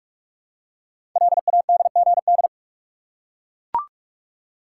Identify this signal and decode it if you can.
Morse code, heard as a keyed tone, about 44 wpm, 700 Hz. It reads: LADGD
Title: Code Groups - 5 Letters - 45wpm